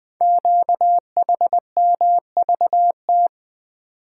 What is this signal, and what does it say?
Morse code, heard as a keyed tone, about 20 wpm, 700 Hz. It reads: QHMVT
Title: Code Groups - 5 Letters - 20wpm